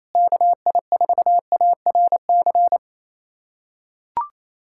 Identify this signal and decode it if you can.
Morse code, heard as a keyed tone, about 28 wpm, 700 Hz. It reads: KI4ARC
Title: USA Callsign DE Callsign 28wpm